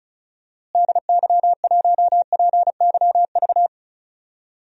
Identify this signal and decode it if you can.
Morse code, heard as a keyed tone, about 34 wpm, 700 Hz. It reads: DY1PYV
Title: DXCC-100 Call Signs - 6 Characters - Repeated 3x - 35wpm